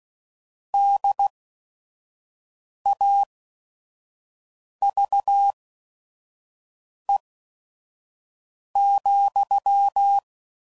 Morse code, heard as a keyed tone, about 16 wpm, spaced as for 5 wpm, 800 Hz. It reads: DAVE,
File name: CW Code Groups @5WPM / QSOs / WD0FBT 1